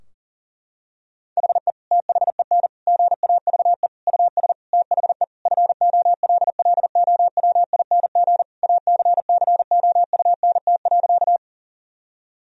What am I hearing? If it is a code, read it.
Morse code, heard as a keyed tone, about 40 wpm, 700 Hz. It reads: HE THEN GAVE US THE FOLLOWING ACCOUNT.
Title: Sentences from Top 1000 Words - Review Part 4 - 40wpm